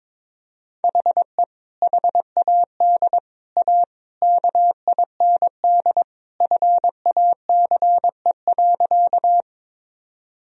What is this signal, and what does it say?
Morse code, heard as a keyed tone, about 22 wpm, 700 Hz. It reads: HE HAD A KIND FACE.
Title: Sentences from Top 1000 Words - Review Part 2 - 22wpm